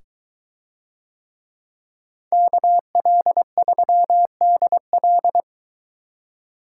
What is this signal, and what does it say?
Morse code, heard as a keyed tone, about 22 wpm, 700 Hz. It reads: KL3DL